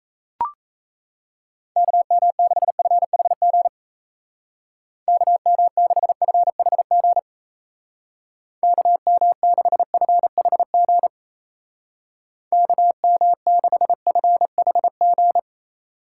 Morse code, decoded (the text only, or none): KM6FHG KM6FHG KM6FHG KM6FHG